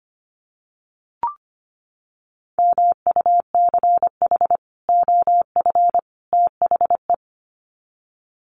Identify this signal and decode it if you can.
Morse code, heard as a keyed tone, about 25 wpm, 700 Hz. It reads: MUCH OF THE
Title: Sets of 3 Words 25wpm